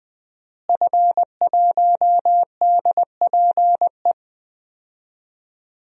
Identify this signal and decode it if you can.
Morse code, heard as a keyed tone, about 20 wpm, 700 Hz. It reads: F1DPE
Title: DXCC-100 Call Signs - 5 Characters - Repeated 3x - 20wpm